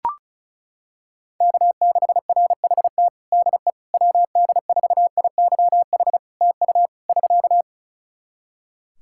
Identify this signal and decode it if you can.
Morse code, heard as a keyed tone, about 36 wpm, 700 Hz. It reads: K6RHT DE WD4IYH TU <SK>
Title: Practice QSO - Exchange 4 - Ending - 35wpm